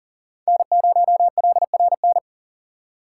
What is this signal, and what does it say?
Morse code, heard as a keyed tone, about 38 wpm, 700 Hz. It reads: N0LRN
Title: USA Call Signs - 5 Characters - 3x Character Spacing - 40wpm